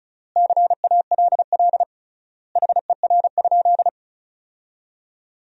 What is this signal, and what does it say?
Morse code, heard as a keyed tone, about 36 wpm, 700 Hz. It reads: CALL HER?
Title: 3x Word-Spacing - Sentences from Top 100 Words - 35wpm